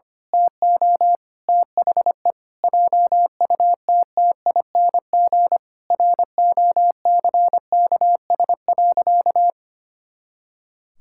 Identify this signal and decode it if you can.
Morse code, heard as a keyed tone, about 25 wpm, 700 Hz. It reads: TO THE JUTTING ROCKS.
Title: Pangram Challenge v2 - 25wpm